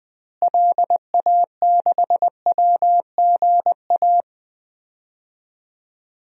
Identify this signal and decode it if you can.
Morse code, heard as a keyed tone, about 20 wpm, 700 Hz. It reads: LA6WGA
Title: DXCC-100 Call Signs - 6 Characters - Repeated 2x - 20wpm